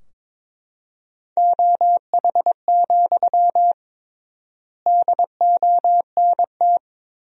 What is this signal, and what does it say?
Morse code, heard as a keyed tone, about 22 wpm, 700 Hz. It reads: OH, DONT
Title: 3x Word-Spacing - Sentences from Top 500 Words - 22wpm